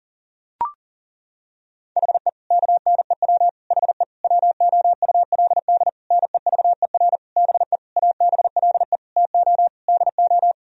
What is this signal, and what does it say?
Morse code, heard as a keyed tone, about 37 wpm, 700 Hz. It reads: HE KNEW HE WOULD NEVER BE ABLE TO DO
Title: Sentences from Top 1000 Words - Review Part 2 - 40wpm